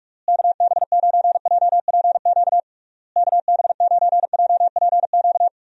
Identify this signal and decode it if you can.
Morse code, heard as a keyed tone, about 41 wpm, 700 Hz. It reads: KB9JPQ KB9JPQ
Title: USA Call Signs - 6 Characters - Repeated 3x - 45wpm